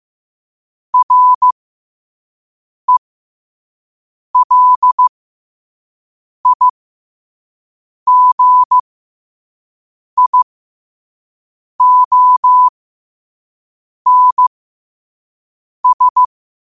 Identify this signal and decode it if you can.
Morse code, heard as a keyed tone, about 15 wpm, spaced as for 5 wpm, 1000 Hz. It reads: RELIGIONS
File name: qotdfiles / 5wpm